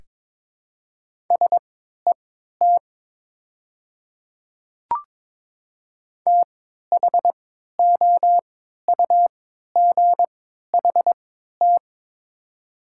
Morse code, heard as a keyed tone, about 22 wpm, spaced as for 13 wpm, 700 Hz. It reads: SET THOUGHT
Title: 3x Character Spacing - Top 300 Words - 22wpm